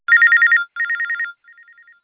There is no code.